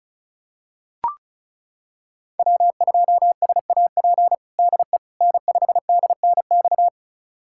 Morse code, heard as a keyed tone, about 36 wpm, 700 Hz. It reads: W2SAP DE N5DNX
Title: USA Callsign DE Callsign - 5 Characters - 35wpm